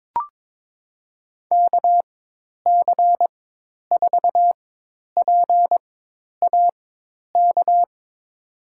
Morse code, heard as a keyed tone, about 22 wpm, spaced as for 10 wpm, 700 Hz. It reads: KC4PAK
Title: USA Call Signs - 6 Characters - 4x Character Spacing - 22wpm